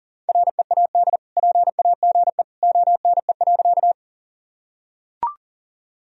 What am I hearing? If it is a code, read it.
Morse code, heard as a keyed tone, about 40 wpm, 700 Hz. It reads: READ PAGE ONE.
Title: Sentences from Top 1000 Words - Review Part 1 - 40wpm